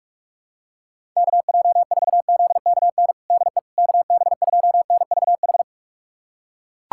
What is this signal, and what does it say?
Morse code, heard as a keyed tone, about 41 wpm, 700 Hz. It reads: KJ4ZKN DE KB2NVH
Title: USA Callsign DE Callsign - 6 Characters - 45wpm